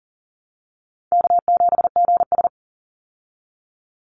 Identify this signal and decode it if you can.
Morse code, heard as a keyed tone, about 40 wpm, 700 Hz. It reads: K7GS
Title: Call Signs Repeated 3x - 40wpm